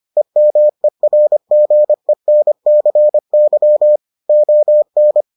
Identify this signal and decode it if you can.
Morse code, heard as a keyed tone, about 25 wpm, 600 Hz. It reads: EMERGENCY ON